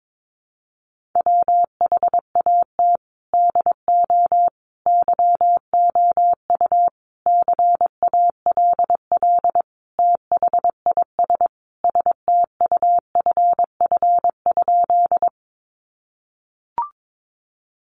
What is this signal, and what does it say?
Morse code, heard as a keyed tone, about 22 wpm, 700 Hz. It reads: WHAT DO YOU CALL THIS STUFF?
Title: Sentences from Top 1400 Words 22wpm